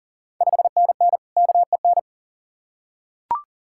Morse code, heard as a keyed tone, about 40 wpm, 690 Hz. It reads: 5NN KEN
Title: Ontario QSO Party Exchange 40wpm